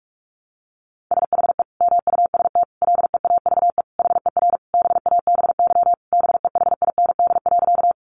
Morse code, heard as a keyed tone, about 44 wpm, 700 Hz. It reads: SHE MUST LEAVE HER BABY BEHIND.
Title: Sentences from Top 1300 Words 45wpm